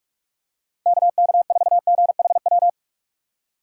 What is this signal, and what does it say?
Morse code, heard as a keyed tone, about 46 wpm, 700 Hz. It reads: KK4GHW
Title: USA Call Signs - 6 Characters - 3x Character Spacing - 45wpm